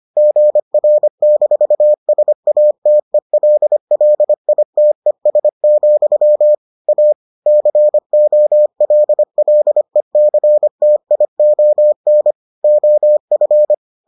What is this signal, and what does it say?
Morse code, heard as a keyed tone, about 25 wpm, 600 Hz. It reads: GR-SATELLITES, A COLLECTION OF